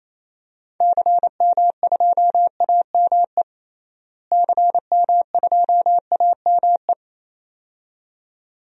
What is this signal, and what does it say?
Morse code, heard as a keyed tone, about 28 wpm, 700 Hz. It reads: CM2AME CM2AME
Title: DXCC-100 Call Signs - 6 Characters - Repeated 2x - 28wpm